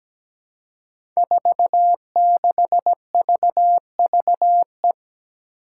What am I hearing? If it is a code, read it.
Morse code, heard as a keyed tone, about 17 wpm, 700 Hz. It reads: 46VVE
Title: Five Challenging Letter-Number Combinations 17wpm